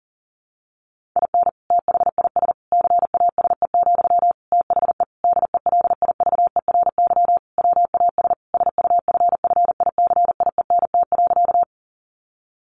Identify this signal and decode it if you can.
Morse code, heard as a keyed tone, about 40 wpm, 700 Hz. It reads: IN THIS CASE, THE DELIVERY WAS SUFFICIENT.
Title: Sentences from Top 1500 Words 40wpm